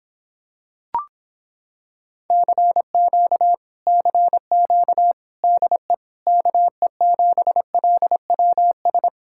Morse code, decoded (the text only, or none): CQ CQ DE KE7LWS